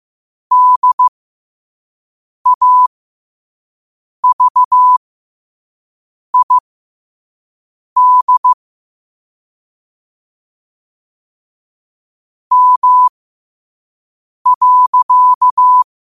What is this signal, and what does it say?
Morse code, heard as a keyed tone, about 15 wpm, spaced as for 5 wpm, 1000 Hz. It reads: DAVID M.
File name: qotdfiles / 5wpm